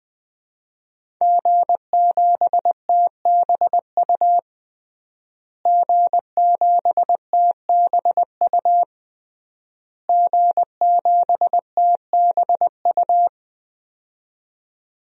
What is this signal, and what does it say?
Morse code, heard as a keyed tone, about 20 wpm, 700 Hz. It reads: G7TBU G7TBU G7TBU
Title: DXCC-100 Call Signs - 5 Characters - Repeated 3x - 20wpm